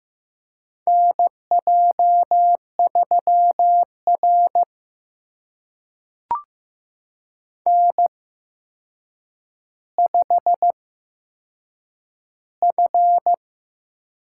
Morse code, heard as a keyed tone, about 15 wpm, spaced as for 7 wpm, 700 Hz. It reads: NJ3R N5F
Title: USA Call Signs - 4 Characters - 8x Character Spacing - 15wpm